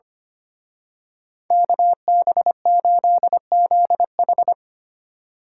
Morse code, heard as a keyed tone, about 29 wpm, 700 Hz. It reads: KB8ZH
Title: Speed Racing - USA Call Signs 25wpm